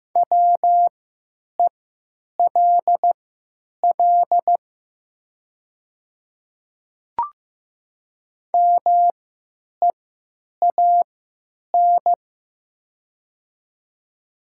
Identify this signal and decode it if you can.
Morse code, heard as a keyed tone, about 15 wpm, spaced as for 9 wpm, 700 Hz. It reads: WELL MEAN
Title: 3x Character Spacing - Top 500 Words - 15wpm